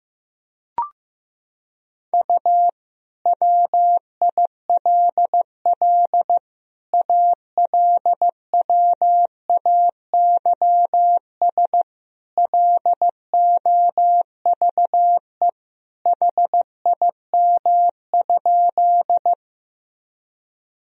Morse code, heard as a keyed tone, about 15 wpm, 700 Hz. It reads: U WILL ALWAYS LOVE HIM?
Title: Sentences from Top 400 Words and Abbreviations 15wpm